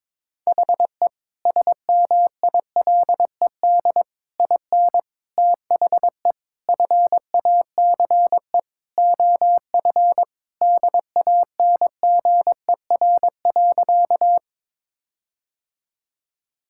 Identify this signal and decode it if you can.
Morse code, heard as a keyed tone, about 22 wpm, 700 Hz. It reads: HE SMILED IN THE FACE OF DANGER.
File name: practice / 22wpm / Sentences with 4-Word Phrases - 22wpm